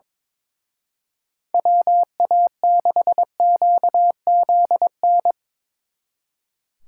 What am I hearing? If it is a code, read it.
Morse code, heard as a keyed tone, about 24 wpm, 700 Hz. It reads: WA6QZN
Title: Speed Racing - USA 2x3 Call Signs 22wpm